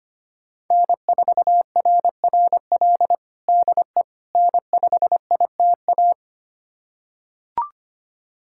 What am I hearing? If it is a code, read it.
Morse code, heard as a keyed tone, about 25 wpm, 700 Hz. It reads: N4RRL DE N5ITA
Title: USA Callsign DE Callsign - 5 Characters - 25wpm